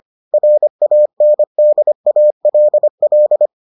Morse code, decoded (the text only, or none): RANDALL